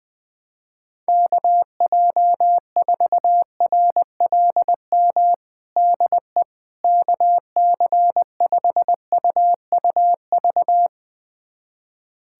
Morse code, heard as a keyed tone, about 20 wpm, 700 Hz. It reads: KJ4RLM DE KC5UUV
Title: USA Callsign DE Callsign - 6 Characters - 20wpm